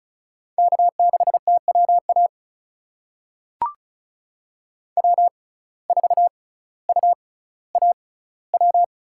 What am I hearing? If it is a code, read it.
Morse code, heard as a keyed tone, about 36 wpm, spaced as for 18 wpm, 700 Hz. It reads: K6TWA W4UAW
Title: USA Call Signs - 5 Characters - 6x Character Spacing - 35wpm